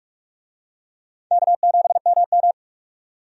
Morse code, heard as a keyed tone, about 46 wpm, 700 Hz. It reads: K7MM
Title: USA Call Signs - 4 Characters - 45wpm